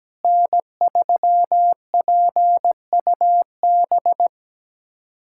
Morse code, heard as a keyed tone, about 17 wpm, 700 Hz. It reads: N3PUB
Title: USA Callsign DE Callsign - 5 Characters - 17wpm